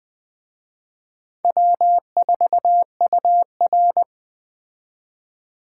Morse code, heard as a keyed tone, about 20 wpm, 700 Hz. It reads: W4UR
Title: Speed Racing - USA Call Signs 17wpm